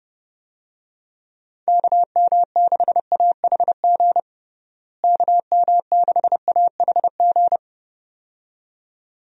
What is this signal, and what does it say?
Morse code, heard as a keyed tone, about 32 wpm, 700 Hz. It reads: KM6AHG KM6AHG
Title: USA Call Signs - 6 Characters - Repeated 2x - 30wpm